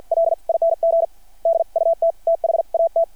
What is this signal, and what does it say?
Morse code, heard as a keyed tone, about 52 wpm, spaced as for 44 wpm, 660 Hz. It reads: LING DUT THAT
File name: archived / CRSnet